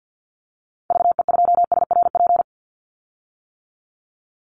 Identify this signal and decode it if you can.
Morse code, heard as a keyed tone, about 50 wpm, 700 Hz. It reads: VE2SRP
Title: Call Signs - 50wpm